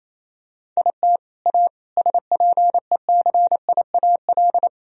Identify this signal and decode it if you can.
Morse code, heard as a keyed tone, about 27 wpm, 700 Hz. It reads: IT A SPECIAL
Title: Sentences from Top 1000 Words - Review Part 3 - 28wpm